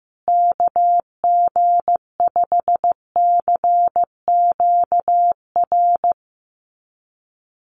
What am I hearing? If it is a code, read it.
Morse code, heard as a keyed tone, about 15 wpm, 700 Hz. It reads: KG5CQR
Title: USA Call Signs Repeated 3x 15wpm